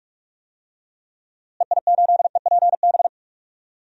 Morse code, heard as a keyed tone, about 46 wpm, 700 Hz. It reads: EI8EPB
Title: DXCC-100 Call Signs - 6 Characters - Repeated 2x - 45wpm